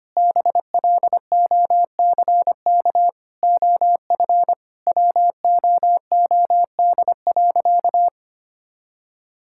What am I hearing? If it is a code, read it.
Morse code, heard as a keyed tone, about 25 wpm, 700 Hz. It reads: BLOCK OF WOOD.
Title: Sentences from Top 1600 Words 25wpm